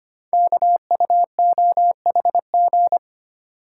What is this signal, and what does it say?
Morse code, heard as a keyed tone, about 25 wpm, 700 Hz. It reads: KUOHG
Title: Code Groups - 5 Letters - 25wpm